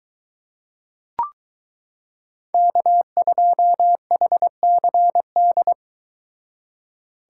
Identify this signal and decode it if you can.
Morse code, heard as a keyed tone, about 22 wpm, 700 Hz. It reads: K2HCD